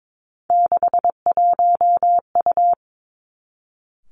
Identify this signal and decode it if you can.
Morse code, heard as a keyed tone, about 22 wpm, 700 Hz. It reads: 61U
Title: Three Challenging Letter-Number Combinations 22wpm